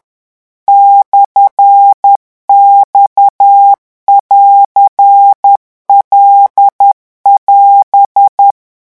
Morse code, heard as a keyed tone, about 10 wpm, 800 Hz. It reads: /X<AR>L<AS>
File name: morse mp3 / revision19-vitesse 10 mots